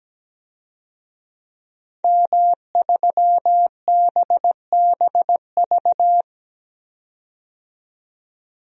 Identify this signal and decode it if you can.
Morse code, heard as a keyed tone, about 17 wpm, 700 Hz. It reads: M3BBV
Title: DXCC-100 Call Signs - 5 Characters - 17wpm